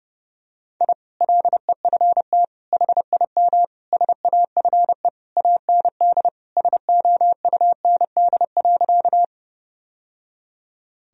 Morse code, heard as a keyed tone, about 30 wpm, 700 Hz. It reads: I LEFT HIM SAFE AND SOUND.
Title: Sentences from Top 1000 Words - Review Part 1 - 30wpm